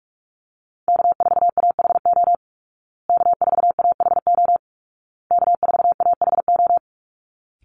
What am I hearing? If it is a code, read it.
Morse code, heard as a keyed tone, about 44 wpm, 700 Hz. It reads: K4AHO K4AHO K4AHO
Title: Call Signs Repeated 2x - 45wpm